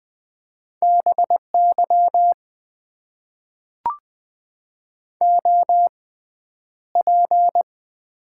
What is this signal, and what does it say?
Morse code, heard as a keyed tone, about 20 wpm, spaced as for 10 wpm, 700 Hz. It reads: BY OP
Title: Top 90 QSO Elements - 6x Character Spacing - 20wpm